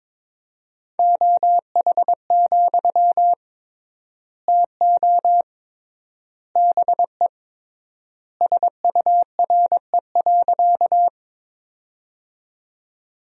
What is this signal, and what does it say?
Morse code, heard as a keyed tone, about 22 wpm, 700 Hz. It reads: OH, TO BE SURE.
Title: 3x Word-Spacing - Sentences from Top 500 Words - 22wpm